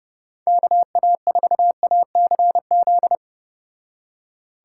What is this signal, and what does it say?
Morse code, heard as a keyed tone, about 30 wpm, 700 Hz. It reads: KA4ACZ